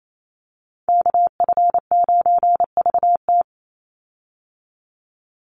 Morse code, heard as a keyed tone, about 28 wpm, 700 Hz. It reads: KF9VT
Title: USA Call Signs 28wpm